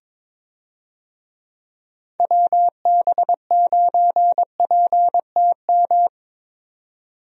Morse code, heard as a keyed tone, about 23 wpm, 700 Hz. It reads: WB9PTM
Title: USA Call Signs - 6 Characters - Repeated 3x - 22wpm